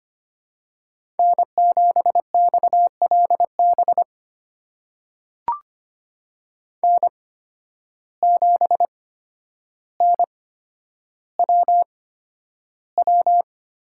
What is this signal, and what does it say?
Morse code, heard as a keyed tone, about 25 wpm, spaced as for 11 wpm, 700 Hz. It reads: N7XLB N7NWW
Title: USA Call Signs - 5 Characters - 8x Character Spacing - 25wpm